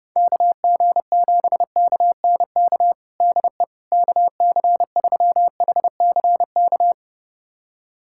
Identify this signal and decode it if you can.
Morse code, heard as a keyed tone, about 32 wpm, 700 Hz. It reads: KG7KNK DE KC3HCK
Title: USA Callsign DE Callsign - 6 Characters - 30wpm